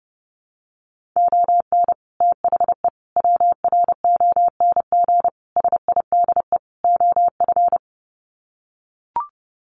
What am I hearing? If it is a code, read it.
Morse code, heard as a keyed tone, about 30 wpm, 700 Hz. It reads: ON THE WRONG SIDE OF
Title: Sets of 5 Words Encore 30wpm